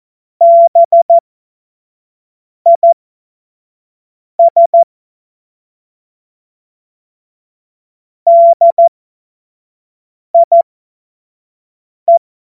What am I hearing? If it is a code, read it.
Morse code, heard as a keyed tone, about 14 wpm, spaced as for 5 wpm, 680 Hz. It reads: BIS DIE
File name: Witz-29-Juli-2025-25-F70